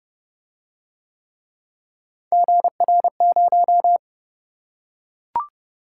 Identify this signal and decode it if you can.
Morse code, heard as a keyed tone, about 31 wpm, 700 Hz. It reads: GR0